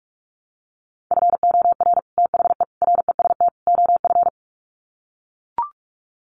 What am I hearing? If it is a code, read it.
Morse code, heard as a keyed tone, about 44 wpm, 700 Hz. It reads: FOR THE REST OF
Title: Sets of 4 Words 45wpm